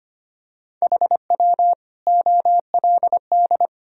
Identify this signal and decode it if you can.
Morse code, heard as a keyed tone, about 25 wpm, 700 Hz. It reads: HW OLD
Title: Sentences from Top 200 Words and Abbreviations 25wpm